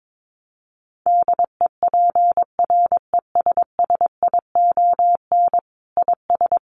Morse code, heard as a keyed tone, about 22 wpm, 700 Hz. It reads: DEPRESSION IS
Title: Sentences from Top 1300 Words 22wpm